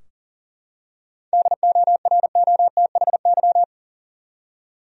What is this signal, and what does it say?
Morse code, heard as a keyed tone, about 40 wpm, 700 Hz. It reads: DOROTHY
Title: USA Common Names - 40wpm